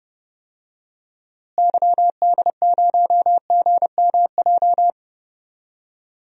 Morse code, heard as a keyed tone, about 30 wpm, 700 Hz. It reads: YD0GMJ
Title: DXCC-100 Call Signs - 6 Characters - Repeated 2x - 30wpm